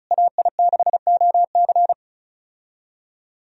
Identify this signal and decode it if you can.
Morse code, heard as a keyed tone, about 34 wpm, 700 Hz. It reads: AI6OC
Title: USA Call Signs - 5 Characters - Repeated 2x - 35wpm